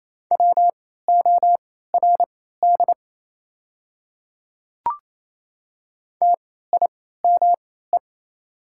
Morse code, heard as a keyed tone, about 28 wpm, spaced as for 16 wpm, 700 Hz. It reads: WORD TIME
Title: 3x Character Spacing - Top 100 Words - 28wpm